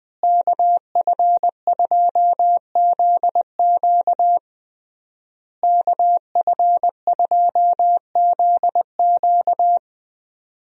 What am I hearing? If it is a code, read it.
Morse code, heard as a keyed tone, about 21 wpm, 700 Hz. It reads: KF2ZQ KF2ZQ
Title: USA Call Signs - 5 Characters - Repeated 3x - 20wpm